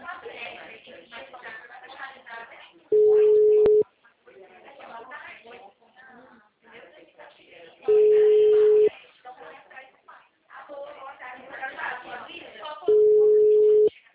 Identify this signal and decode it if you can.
Ringback tone (Europe, 425 Hz), with speech.